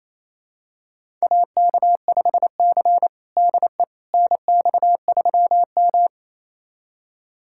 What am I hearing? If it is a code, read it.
Morse code, heard as a keyed tone, about 27 wpm, 690 Hz. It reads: AK5C DE NX3M
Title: USA Callsign DE Callsign - 4 Characters - 28wpm